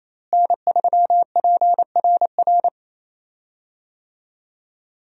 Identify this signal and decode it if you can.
Morse code, heard as a keyed tone, about 28 wpm, 700 Hz. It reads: N3PRR
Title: USA Call Signs - 5 Characters - Repeated 3x - 28wpm